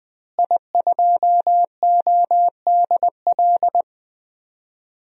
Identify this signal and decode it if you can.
Morse code, heard as a keyed tone, about 20 wpm, 700 Hz. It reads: I2ODL
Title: DXCC-100 Call Signs - 5 Characters - Repeated 3x - 20wpm